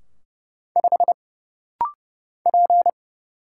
Morse code, heard as a keyed tone, about 30 wpm, 700 Hz. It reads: 5 P
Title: Single Letter-Number - Rapid-Fire 30wpm